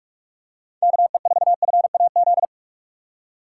Morse code, heard as a keyed tone, about 46 wpm, 700 Hz. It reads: KE4FAZ
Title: USA Call Signs - 6 Characters - 4x Character Spacing - 45wpm